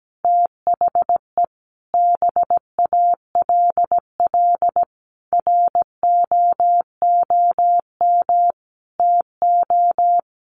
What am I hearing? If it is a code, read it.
Morse code, heard as a keyed tone, about 17 wpm, 700 Hz. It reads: THE BALL ROOM TO
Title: Sentences from Top 1300 Words 17wpm